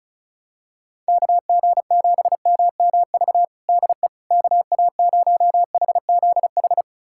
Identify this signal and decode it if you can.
Morse code, heard as a keyed tone, about 34 wpm, 690 Hz. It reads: KG7MMV DE KA0HZH
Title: USA Callsign DE Callsign 35wpm